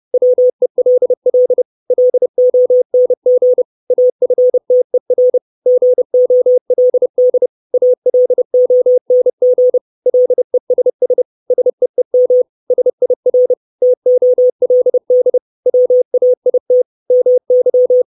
Morse code, heard as a keyed tone, about 30 wpm, 500 Hz. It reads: WELL LONG AFTER GOLD ALONG LESS SEEM SIR TOLD WAIT MY
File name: 632 Words 30WPM